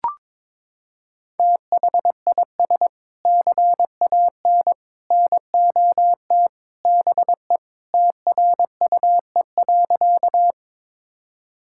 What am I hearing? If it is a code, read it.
Morse code, heard as a keyed tone, about 22 wpm, 700 Hz. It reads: THIS CAN NOT BE TRUE.
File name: practice / 22wpm / Sentences with 3-Word Phrases - 22wpm